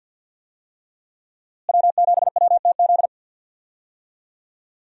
Morse code, heard as a keyed tone, about 52 wpm, 700 Hz. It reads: W7WTZ
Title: USA Call Signs - 5 Characters - Repeated 3x - 50wpm